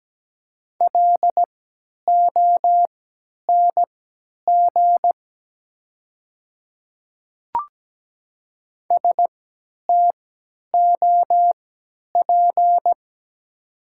Morse code, heard as a keyed tone, about 17 wpm, spaced as for 10 wpm, 700 Hz. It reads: LONG STOP
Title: 3x Character Spacing - Top 300 Words - 17wpm